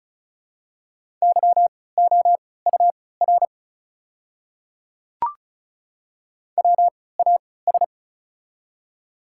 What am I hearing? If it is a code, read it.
Morse code, heard as a keyed tone, about 36 wpm, spaced as for 20 wpm, 700 Hz. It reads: YOUR WAS
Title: 3x Character Spacing - Top 300 Words - 35wpm